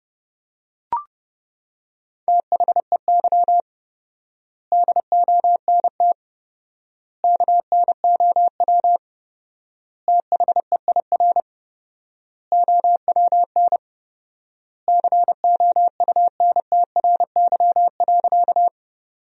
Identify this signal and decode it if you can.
Morse code, heard as a keyed tone, about 30 wpm, 700 Hz. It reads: THEY DONT KNOW THEIR OWN COUNTRY.
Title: 4x Word-Spacing - Sentences from Top 300 Words - 30wpm